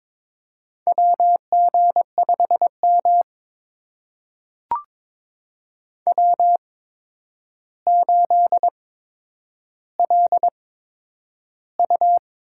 Morse code, heard as a keyed tone, about 22 wpm, spaced as for 9 wpm, 700 Hz. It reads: WG5M W8LU
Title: USA Call Signs - 4 Characters - 8x Character Spacing - 22wpm